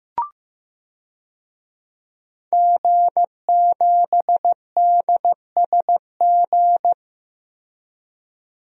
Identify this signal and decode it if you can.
Morse code, heard as a keyed tone, about 15 wpm, 700 Hz. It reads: G7DSG
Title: DXCC-100 Call Signs - 5 Characters - 15wpm